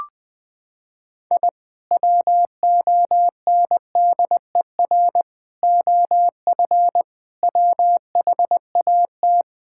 Morse code, heard as a keyed tone, about 20 wpm, 700 Hz. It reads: I WONDER OF WHAT
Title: Sentences with 2-Word Phrases - 20wpm